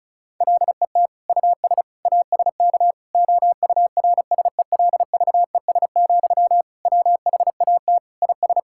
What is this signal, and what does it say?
Morse code, heard as a keyed tone, about 36 wpm, 700 Hz. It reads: LET US ASK OURSELVES, WHAT IS
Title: Sentences from Top 1600 Words 35wpm